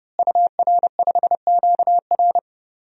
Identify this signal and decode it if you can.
Morse code, heard as a keyed tone, about 30 wpm, 700 Hz. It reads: UR5QR
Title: DXCC-100 Call Signs - 5 Characters - Repeated 3x - 30wpm